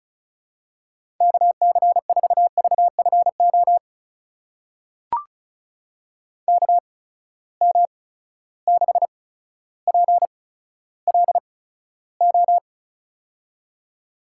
Morse code, heard as a keyed tone, about 36 wpm, spaced as for 15 wpm, 700 Hz. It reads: KC4VFO KM6PLO
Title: USA Call Signs - 8x Character Spacing - 35wpm